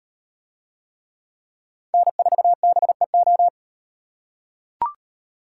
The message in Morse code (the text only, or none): N4BEO